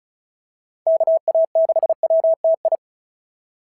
Morse code, heard as a keyed tone, about 31 wpm, 660 Hz. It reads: KA6WTI
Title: SOTA QSO Practice 35wpm